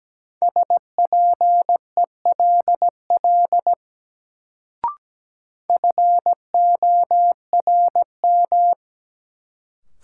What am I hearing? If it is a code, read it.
Morse code, heard as a keyed tone, about 17 wpm, 700 Hz. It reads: SPELL FORM
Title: Sentences from Top 1000 Words - Review Part 1 - New Word Drill - 17wpm